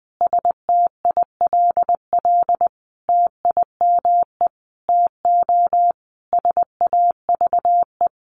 Morse code, heard as a keyed tone, about 20 wpm, 700 Hz. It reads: STILL TIME TO SAVE
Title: Sentences from Top 1100 Words 20wpm